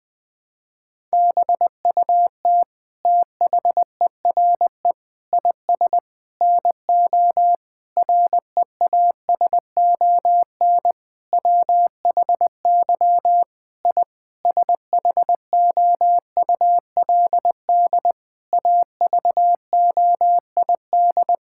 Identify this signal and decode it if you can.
Morse code, heard as a keyed tone, about 20 wpm, 700 Hz. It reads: BUT THERE IS NO REASON WHY I SHOULD AVOID
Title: Sentences from Top 1000 Words - Review Part 6 - 20wpm